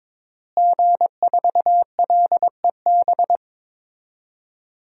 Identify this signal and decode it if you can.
Morse code, heard as a keyed tone, about 22 wpm, 700 Hz. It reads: G4LEB
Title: DXCC-100 Call Signs - 5 Characters - Repeated 3x - 22wpm